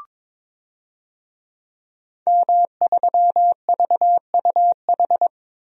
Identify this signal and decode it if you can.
Morse code, heard as a keyed tone, about 22 wpm, 700 Hz. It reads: M3VUH
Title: DXCC-100 Call Signs - 5 Characters - 22wpm